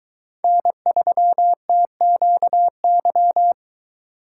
Morse code, heard as a keyed tone, about 22 wpm, 700 Hz. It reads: N3TQY